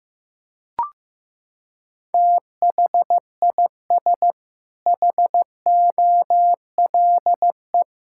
Morse code, heard as a keyed tone, about 15 wpm, 700 Hz. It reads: THIS HOLE
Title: Sentences from Top 1600 Words 15wpm